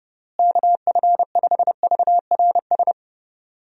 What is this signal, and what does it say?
Morse code, heard as a keyed tone, about 30 wpm, 700 Hz. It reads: KF5VRS
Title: USA Call Signs - 6 Characters - Repeated 2x - 30wpm